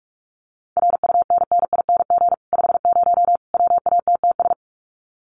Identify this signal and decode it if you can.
Morse code, heard as a keyed tone, about 44 wpm, 700 Hz. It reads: RUNNING 50 WATTS
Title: Common QSO Phrases 45wpm